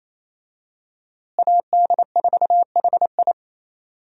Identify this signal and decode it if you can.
Morse code, heard as a keyed tone, about 28 wpm, 700 Hz. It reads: AD4HI
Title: USA Call Signs - 5 Characters - Repeated 3x - 28wpm